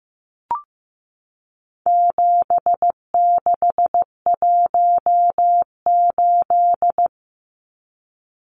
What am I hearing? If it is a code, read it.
Morse code, heard as a keyed tone, about 15 wpm, 700 Hz. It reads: 7618
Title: Numbers - 4-Digit 15wpm